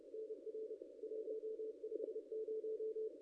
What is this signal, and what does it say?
Morse code, heard as a keyed tone, about 30 wpm, 440 Hz. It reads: NR 150
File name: contests / ss-cw-k5tr-op-n5rz -0200